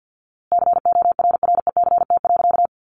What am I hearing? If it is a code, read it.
Morse code, heard as a keyed tone, about 50 wpm, 700 Hz. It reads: CORRECT.
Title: A Sound Of Thunder 50wpm